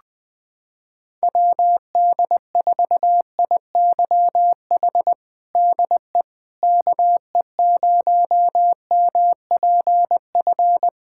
Morse code, heard as a keyed tone, about 20 wpm, 700 Hz. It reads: WD4IYH DE KE0MPF
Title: Practice QSO - Exchange 4 - Ending - 20wpm